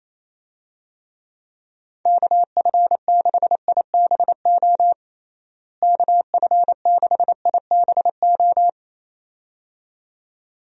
Morse code, heard as a keyed tone, about 27 wpm, 700 Hz. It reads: KF6IBO KF6IBO
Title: USA Call Signs - 6 Characters - Repeated 2x - 28wpm